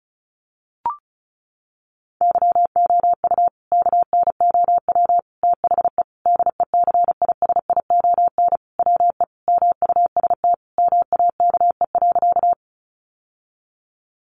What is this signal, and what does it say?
Morse code, heard as a keyed tone, about 34 wpm, 700 Hz. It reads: YOU KNOW THE DECISION WE MUST MAKE.
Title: Sentences from Top 1000 Words 35wpm